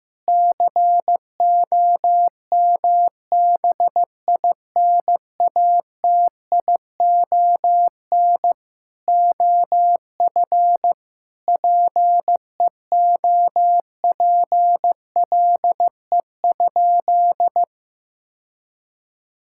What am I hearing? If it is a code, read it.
Morse code, heard as a keyed tone, about 15 wpm, 700 Hz. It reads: COMBINATION OF PEOPLE?
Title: Sentences from Top 1000 Words - Review Part 6 - 15wpm